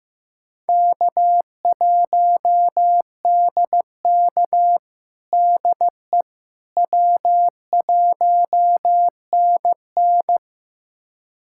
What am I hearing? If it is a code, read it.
Morse code, heard as a keyed tone, about 15 wpm, 700 Hz. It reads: K1DK DE W1NN
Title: USA Callsign DE Callsign - 4 Characters - 15wpm